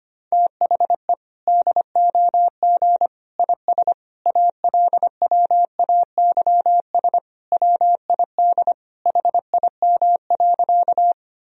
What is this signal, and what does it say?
Morse code, heard as a keyed tone, about 25 wpm, 700 Hz. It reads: THE DOG IS ALWAYS WID HIM.
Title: Sentences from Top 400 Words and Abbreviations 25wpm